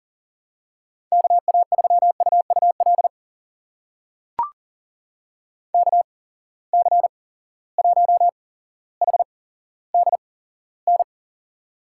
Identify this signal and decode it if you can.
Morse code, heard as a keyed tone, about 38 wpm, spaced as for 17 wpm, 700 Hz. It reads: KA3UUL KC1HDN
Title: USA Call Signs - 8x Character Spacing - 40wpm